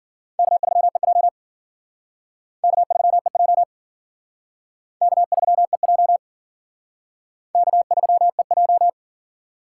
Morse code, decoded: K3EJ K3EJ K3EJ K3EJ

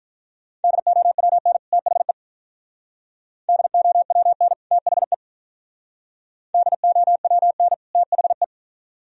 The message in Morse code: DOWN THE DOWN THE DOWN THE

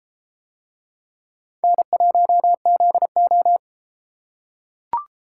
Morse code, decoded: N1ZO

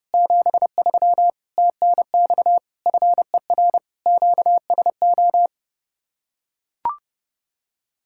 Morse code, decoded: 73 TNX FER QSO

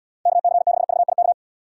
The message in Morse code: KC6VIK

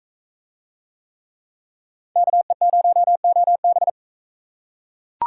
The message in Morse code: KE0OB

